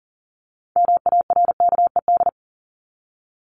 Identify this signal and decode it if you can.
Morse code, heard as a keyed tone, about 40 wpm, 700 Hz. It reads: MARKED
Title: ED Suffix - Top 100 Words with Base Words 40wpm